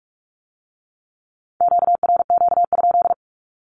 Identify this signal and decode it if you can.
Morse code, heard as a keyed tone, about 46 wpm, 700 Hz. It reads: QRQ?
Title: Q-Codes with Spoken Meaning - 45wpm